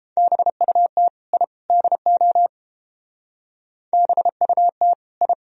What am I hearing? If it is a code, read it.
Morse code, heard as a keyed tone, about 31 wpm, 700 Hz. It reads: BUT I DO BUT I